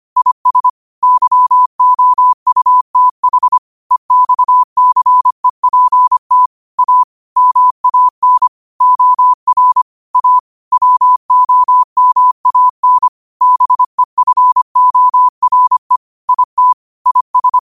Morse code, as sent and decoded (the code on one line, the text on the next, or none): .. ... / -.-- --- ..- - .... / . -..- -.-. .--. - / .- / -- .- -. / --- .-. / .- / .-- --- -- .- -. / -... ..-. --- .-. / .. - / .. ...
IS YOUTH EXCEPT A MAN OR A WOMAN BEFORE IT IS